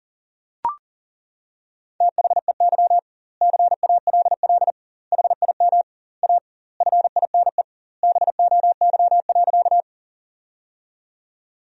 Morse code, heard as a keyed tone, about 37 wpm, 700 Hz. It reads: THEY CALL HIM A FINE BOY.
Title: 2x Word-Spacing - Sentences from Top 500 Words - 40wpm